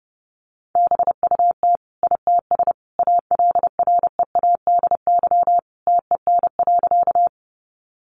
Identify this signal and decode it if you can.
Morse code, heard as a keyed tone, about 29 wpm, 700 Hz. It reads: BUT ITS ALREADY TEN.
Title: Sentences from Top 700 Words 30wpm